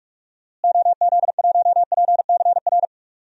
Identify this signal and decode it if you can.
Morse code, heard as a keyed tone, about 42 wpm, 700 Hz. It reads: OZ1PCR
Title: DXCC-100 Call Signs - 6 Characters - Repeated 2x - 45wpm